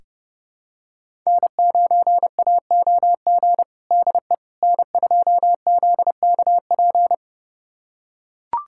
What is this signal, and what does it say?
Morse code, heard as a keyed tone, about 30 wpm, 700 Hz. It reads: N9AOG DE N2ZKP
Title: USA Callsign DE Callsign - 5 Characters - 30wpm